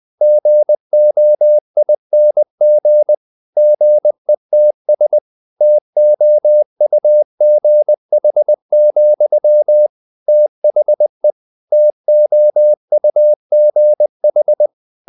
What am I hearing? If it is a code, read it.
Morse code, heard as a keyed tone, about 20 wpm, 600 Hz. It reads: GOING GETS TOUGH, THE TOUGH
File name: CW / Practice / EnglishSayings20